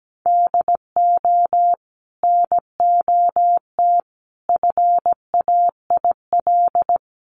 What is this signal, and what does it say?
Morse code, heard as a keyed tone, about 17 wpm, 700 Hz. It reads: DO NOT FAIL
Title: Sentences from Top 1100 Words 17wpm